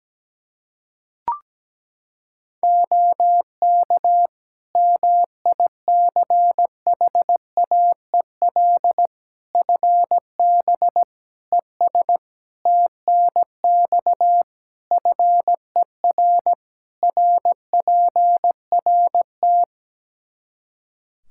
Morse code, heard as a keyed tone, about 17 wpm, 700 Hz. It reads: OK MICHAEL FB ES TNX FER RPRT
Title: K1USN SST QSO Practice 17wpm